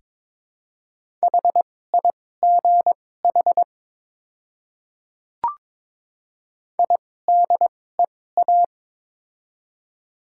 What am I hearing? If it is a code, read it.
Morse code, heard as a keyed tone, about 22 wpm, spaced as for 16 wpm, 700 Hz. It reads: HIGH IDEA
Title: 2x Character Spacing - Top 300 Words - 22wpm